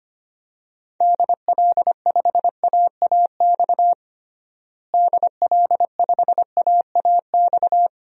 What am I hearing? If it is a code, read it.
Morse code, heard as a keyed tone, about 25 wpm, 700 Hz. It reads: DL5AAX DL5AAX
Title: DXCC-100 Call Signs - 6 Characters - Repeated 2x - 25wpm